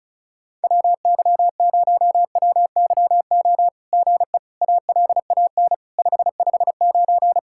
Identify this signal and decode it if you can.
Morse code, heard as a keyed tone, about 36 wpm, 700 Hz. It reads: WY0WYO GE ALAN 559
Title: SOTA QSO Practice 35wpm